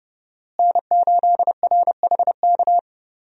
Morse code, heard as a keyed tone, about 32 wpm, 700 Hz. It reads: N8RHK
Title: USA Call Signs - 5 Characters - Repeated 3x - 30wpm